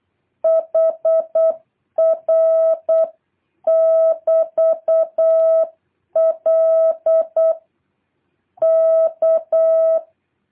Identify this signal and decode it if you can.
Morse code, heard as a keyed tone, about 8 wpm, 650 Hz. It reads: HR <BT>L K
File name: jh8cbh / 2023cw